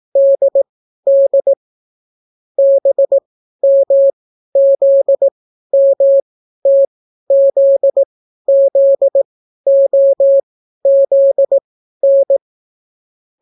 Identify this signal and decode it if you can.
Morse code, heard as a keyed tone, about 18 wpm, spaced as for 12 wpm, 550 Hz. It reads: DD BMZMTZZOZN